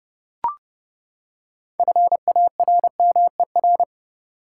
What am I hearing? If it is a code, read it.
Morse code, heard as a keyed tone, about 30 wpm, 700 Hz. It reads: FARMER
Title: Six Letter Words - 30wpm